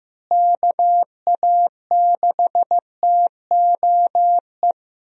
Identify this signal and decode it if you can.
Morse code, heard as a keyed tone, about 15 wpm, 700 Hz. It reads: KA6TOE
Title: USA Call Signs - 6 Characters - Repeated 2x - 15wpm